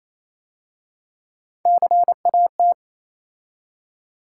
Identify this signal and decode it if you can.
Morse code, heard as a keyed tone, about 27 wpm, 690 Hz. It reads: CAT
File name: Easy Words - Speed-Racing with Hints - 22wpm